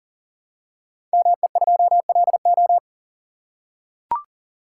Morse code, heard as a keyed tone, about 41 wpm, 690 Hz. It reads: ME2LO
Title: Code Groups - 5 Letter:Numbers - 40wpm